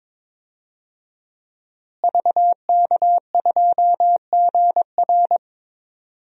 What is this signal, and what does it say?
Morse code, heard as a keyed tone, about 22 wpm, 700 Hz. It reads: VK2GR
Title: DXCC-100 Call Signs - 5 Characters - Repeated 3x - 22wpm